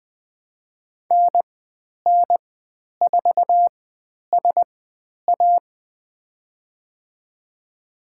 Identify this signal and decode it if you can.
Morse code, heard as a keyed tone, about 20 wpm, spaced as for 10 wpm, 700 Hz. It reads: NN4SA
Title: 109 Lesson 18g - Letter m - Call Signs 20wpm